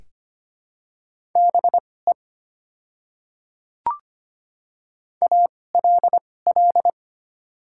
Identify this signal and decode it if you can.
Morse code, heard as a keyed tone, about 25 wpm, spaced as for 18 wpm, 700 Hz. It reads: BE ALL
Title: 2x Character Spacing - Top 100 Words - 25wpm